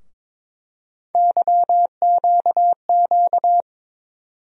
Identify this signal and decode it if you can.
Morse code, heard as a keyed tone, about 21 wpm, 700 Hz. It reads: YQQ
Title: USA Call Sign Postfixes - 3 Letters 22wpm